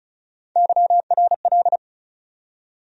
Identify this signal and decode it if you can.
Morse code, heard as a keyed tone, about 34 wpm, 700 Hz. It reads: YRL
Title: USA Call Sign Postfixes - 3 Letters 35wpm